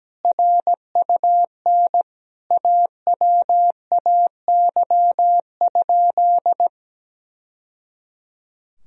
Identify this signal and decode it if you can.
Morse code, heard as a keyed tone, about 17 wpm, 700 Hz. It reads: RUN AWAY?
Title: Sentences from Top 1000 Words - Review Part 1 - 17wpm